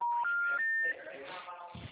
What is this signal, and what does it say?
Special information tone, with speech.
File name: out- -2032